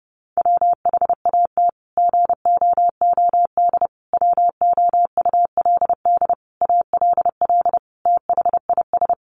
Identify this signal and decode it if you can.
Morse code, heard as a keyed tone, about 30 wpm, 700 Hz. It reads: WHAT GOOD WOULD ALL THIS
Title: Sentences from Top 1200 Words 30wpm